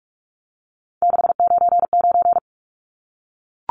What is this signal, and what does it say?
Morse code, heard as a keyed tone, about 44 wpm, 700 Hz. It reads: 699
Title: Numbers - 3-Digit 45wpm